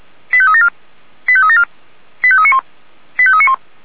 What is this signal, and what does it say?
Five-tone selective call (ZVEI), heard as quick runs of five tones, tones of about 70 ms. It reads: 85374 85374 85391 85391